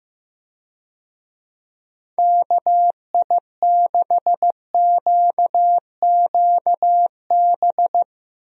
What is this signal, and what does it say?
Morse code, heard as a keyed tone, about 15 wpm, 700 Hz. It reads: KI6QQB
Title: USA Call Signs - 6 Characters - 15wpm